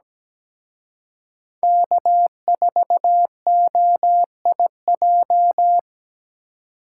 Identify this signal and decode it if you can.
Morse code, heard as a keyed tone, about 20 wpm, 700 Hz. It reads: K4OIJ
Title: Speed Racing - USA Call Signs 17wpm